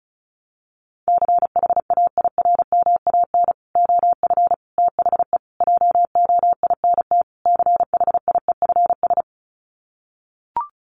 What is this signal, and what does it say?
Morse code, heard as a keyed tone, about 34 wpm, 700 Hz. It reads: CHAIRMAN OF THE JOINT CHIEFS
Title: Sets of 5 Words Encore 35wpm